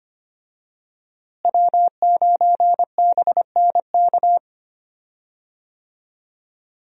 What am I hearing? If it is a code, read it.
Morse code, heard as a keyed tone, about 25 wpm, 700 Hz. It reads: W9BNK
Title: USA Call Signs - 5 Characters - Repeated 2x - 25wpm